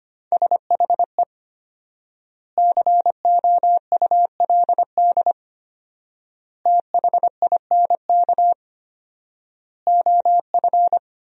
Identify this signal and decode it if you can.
Morse code, heard as a keyed tone, about 25 wpm, 700 Hz. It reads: SHE COULD THINK OF
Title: 4x Word-Spacing - Sentences from Top 500 Words - 25wpm